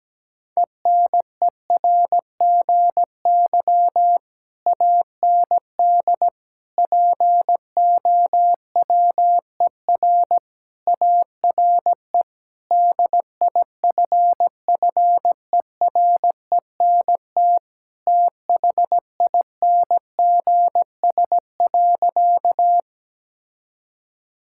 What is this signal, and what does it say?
Morse code, heard as a keyed tone, about 17 wpm, 700 Hz. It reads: ENERGY AND POWER ARE DIFFERENT THINGS.
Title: Sentences from Top 1000 Words - Review Part 6 - 17wpm